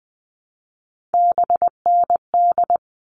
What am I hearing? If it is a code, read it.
Morse code, heard as a keyed tone, about 20 wpm, 700 Hz. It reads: BND
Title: All Cut Numbers - 3-Digit 20wpm